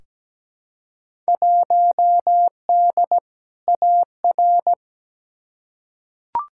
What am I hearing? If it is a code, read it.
Morse code, heard as a keyed tone, about 17 wpm, 700 Hz. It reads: 1D AR
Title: Field Day Exchanges Encore - 17wpm